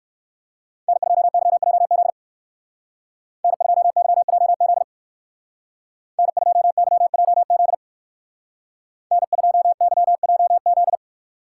Morse code, heard as a keyed tone, about 59 wpm, 700 Hz. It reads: N2YJZ N2YJZ N2YJZ N2YJZ